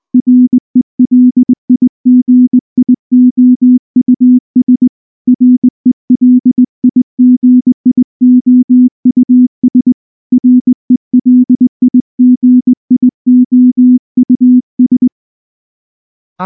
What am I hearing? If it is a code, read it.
Morse code, heard as a keyed tone, about 19 wpm, 250 Hz. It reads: RELIGIOUS RELIGIOUS RELIGIOUS
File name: morse / morse-training-20WPM-3